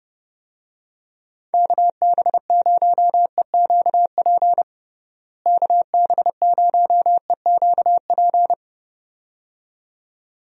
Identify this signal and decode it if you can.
Morse code, heard as a keyed tone, about 30 wpm, 700 Hz. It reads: KB0EQP KB0EQP
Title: USA Call Signs - 6 Characters - Repeated 2x - 30wpm